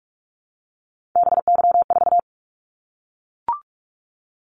Morse code, heard as a keyed tone, about 44 wpm, 700 Hz. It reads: BY4
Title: Callsign Prefixes for Top 100 DXCC - 45wpm